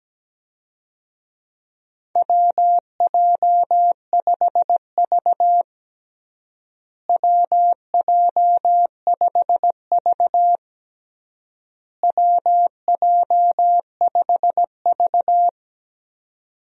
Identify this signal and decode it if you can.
Morse code, heard as a keyed tone, about 17 wpm, 700 Hz. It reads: WJ5V WJ5V WJ5V
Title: USA Call Signs - 4 Characters - Repeated 3x - 17wpm